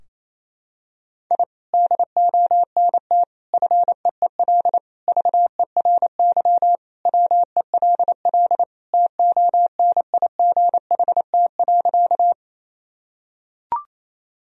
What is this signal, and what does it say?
Morse code, heard as a keyed tone, about 28 wpm, 700 Hz. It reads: I DONT FEEL VERY WELL TONIGHT.
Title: Sentences from Top 1000 Words - Review Part 3 - 28wpm